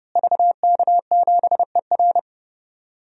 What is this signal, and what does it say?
Morse code, heard as a keyed tone, about 30 wpm, 700 Hz. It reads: VK7ER
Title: DXCC-100 Call Signs - 5 Characters - Repeated 2x - 30wpm